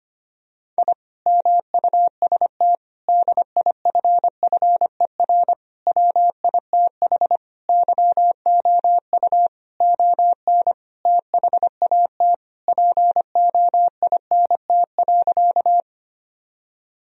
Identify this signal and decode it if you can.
Morse code, heard as a keyed tone, about 25 wpm, 700 Hz. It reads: I MUST DIFFER WITH YOU ON THAT POINT.
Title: Sentences from Top 1000 Words - Review Part 1 - 25wpm